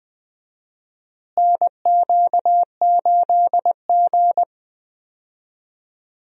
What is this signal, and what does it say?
Morse code, heard as a keyed tone, about 20 wpm, 700 Hz. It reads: NQ8G
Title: USA Call Signs - 4 Characters - Repeated 2x - 20wpm